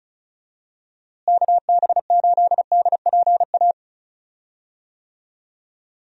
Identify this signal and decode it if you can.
Morse code, heard as a keyed tone, about 36 wpm, 700 Hz. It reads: KB8DPA
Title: USA Call Signs - 6 Characters - Repeated 2x - 35wpm